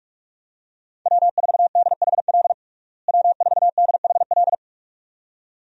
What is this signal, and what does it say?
Morse code, heard as a keyed tone, about 43 wpm, 700 Hz. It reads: W4DHL W4DHL
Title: USA Call Signs - 5 Characters - Repeated 2x - 45wpm